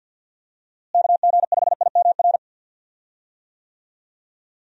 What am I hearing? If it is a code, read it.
Morse code, heard as a keyed tone, about 51 wpm, 700 Hz. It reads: KG5IMR
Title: USA Call Signs - 6 Characters - Repeated 2x - 50wpm